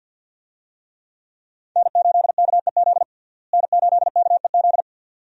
Morse code, heard as a keyed tone, about 51 wpm, 700 Hz. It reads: N8KEZ N8KEZ
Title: USA Call Signs - 5 Characters - Repeated 2x - 50wpm